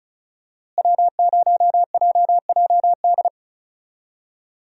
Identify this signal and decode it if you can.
Morse code, heard as a keyed tone, about 36 wpm, 700 Hz. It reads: W0JJD
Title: USA Call Signs - 5 Characters - Repeated 3x - 35wpm